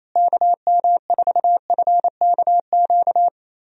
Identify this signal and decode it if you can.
Morse code, heard as a keyed tone, about 28 wpm, 700 Hz. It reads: KM4FKQ
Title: USA Call Signs - 6 Characters - Repeated 2x - 28wpm